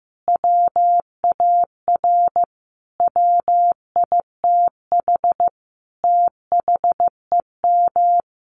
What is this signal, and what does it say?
Morse code, heard as a keyed tone, about 15 wpm, 700 Hz. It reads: WAR WITH THEM
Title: Sentences from Top 1300 Words 15wpm